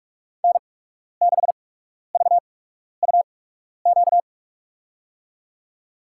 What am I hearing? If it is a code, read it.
Morse code, heard as a keyed tone, about 42 wpm, spaced as for 12 wpm, 690 Hz. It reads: N6VUQ
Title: USA Call Signs - 5 Characters - 8x Character Spacing - 45wpm